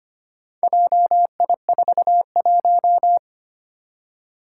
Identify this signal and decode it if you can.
Morse code, heard as a keyed tone, about 25 wpm, 700 Hz. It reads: JI41
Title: Four Challenging Letter-Number Combinations 25wpm